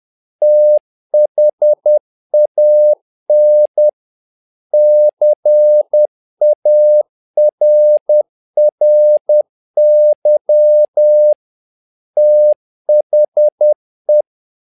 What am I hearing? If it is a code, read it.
Morse code, heard as a keyed tone, about 10 wpm, 600 Hz. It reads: THAN CARRY THE
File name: learning center / tower trouble / tower trouble 10wpm